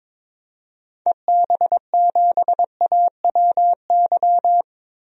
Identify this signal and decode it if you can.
Morse code, heard as a keyed tone, about 21 wpm, 700 Hz. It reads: EB7AWY
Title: DXCC-100 Call Signs - 6 Characters - Repeated 2x - 22wpm